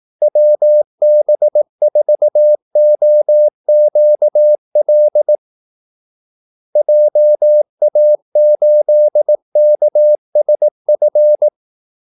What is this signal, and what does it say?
Morse code, heard as a keyed tone, about 18 wpm, 600 Hz. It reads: WB4OQL JA8KSF
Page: Kaldesignaler 18 wpm